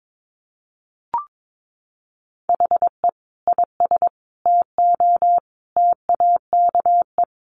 Morse code, heard as a keyed tone, about 22 wpm, 700 Hz. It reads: HE IS TO TAKE
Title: Sentences from Top 1400 Words 22wpm